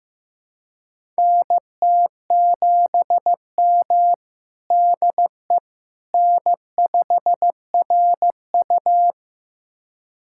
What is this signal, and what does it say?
Morse code, heard as a keyed tone, about 15 wpm, 700 Hz. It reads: NT7M DE N5RU
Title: USA Callsign DE Callsign - 4 Characters - 15wpm